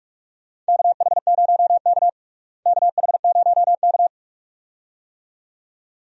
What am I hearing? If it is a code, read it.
Morse code, heard as a keyed tone, about 46 wpm, 700 Hz. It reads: KH0K KH0K
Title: USA Call Signs - 4 Characters - Repeated 2x - 45wpm